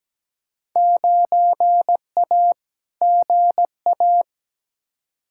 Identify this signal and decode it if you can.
Morse code, heard as a keyed tone, about 17 wpm, 700 Hz. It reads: 9A GA
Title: Field Day Exchanges Encore - 17wpm